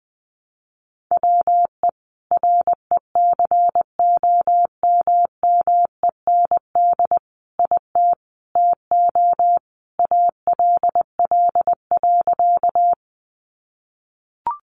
Morse code, heard as a keyed tone, about 20 wpm, 700 Hz. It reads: WE RECOMMEND IT TO ALL.
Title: Sentences from Top 1200 Words 20wpm